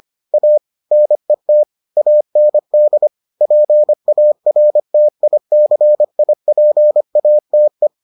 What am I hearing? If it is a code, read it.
Morse code, heard as a keyed tone, about 25 wpm, 600 Hz. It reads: A NET AND PARTICIPATE